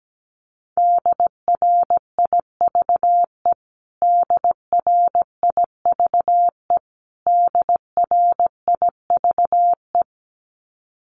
Morse code, heard as a keyed tone, about 17 wpm, 700 Hz. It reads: DRIVE DRIVE DRIVE
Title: Top 500 Words Repeated 17wpm